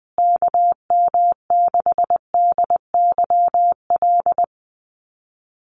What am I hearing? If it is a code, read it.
Morse code, heard as a keyed tone, about 19 wpm, 700 Hz. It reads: KM6DYL
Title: USA Call Signs Repeated 3x 20wpm